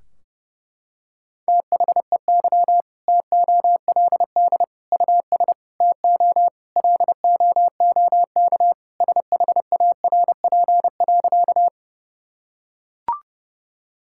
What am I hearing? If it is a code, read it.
Morse code, heard as a keyed tone, about 30 wpm, 700 Hz. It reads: THEY TOLD US TO LOOK SHARP.
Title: Sentences from Top 1600 Words 30wpm